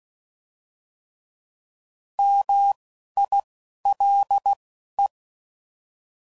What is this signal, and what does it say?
Morse code, heard as a keyed tone, about 16 wpm, spaced as for 11 wpm, 800 Hz. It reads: MILE